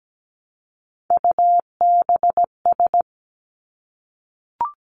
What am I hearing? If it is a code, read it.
Morse code, heard as a keyed tone, about 17 wpm, 700 Hz. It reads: UBS
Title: Three Challenging Letter-Number Combinations 17wpm